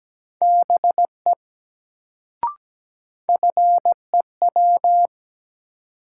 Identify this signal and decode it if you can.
Morse code, heard as a keyed tone, about 17 wpm, 700 Hz. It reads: BE FEW
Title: Sets of 2 Words - New Word Drill - 17wpm